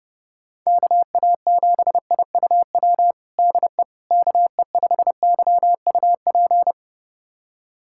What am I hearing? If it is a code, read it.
Morse code, heard as a keyed tone, about 28 wpm, 700 Hz. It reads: KA7IUW DE KE5YUP
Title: USA Callsign DE Callsign - 6 Characters - 30wpm